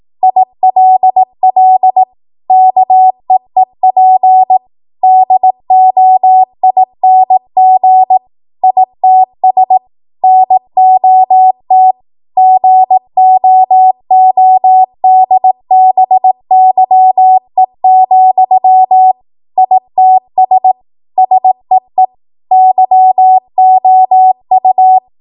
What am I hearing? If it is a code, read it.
Morse code, heard as a keyed tone, about 18 wpm, 750 Hz. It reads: ILL KEEP DOING ITS NOT GOODBYE, ITS SEE YOU
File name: Archive / 18 WPM / 18WPM